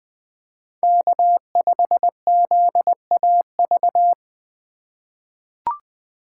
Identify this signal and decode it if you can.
Morse code, heard as a keyed tone, about 20 wpm, 700 Hz. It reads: K5ZAV
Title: USA Call Signs - 4x Character Spacing - 20wpm